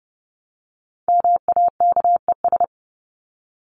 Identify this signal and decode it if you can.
Morse code, heard as a keyed tone, about 30 wpm, 700 Hz. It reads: MAKES
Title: ES:S Suffix - Top 100 Words with Base Words 30wpm